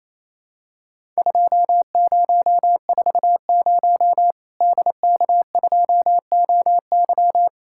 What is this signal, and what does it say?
Morse code, heard as a keyed tone, about 28 wpm, 700 Hz. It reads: 2040 DK2OY
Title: Worked All Europe DX Contest - QTC Exchange - 28wpm